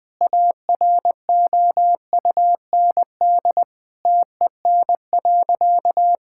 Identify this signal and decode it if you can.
Morse code, heard as a keyed tone, about 21 wpm, 700 Hz. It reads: AROUND TEN.
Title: Sentences with 6-Word Phrases - Spoken Beginning - 20wpm